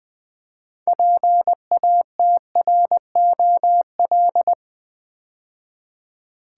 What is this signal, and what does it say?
Morse code, heard as a keyed tone, about 20 wpm, 700 Hz. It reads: PATROL
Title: Build A Word - 2 Letters At A Time - 6-Letter Words - 20wpm